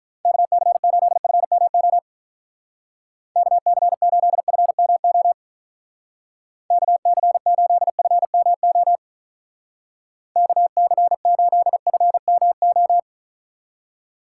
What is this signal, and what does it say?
Morse code, heard as a keyed tone, about 42 wpm, 700 Hz. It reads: KC8FMO KC8FMO KC8FMO KC8FMO
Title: Speed Racing - USA 2x3 Call Signs 35wpm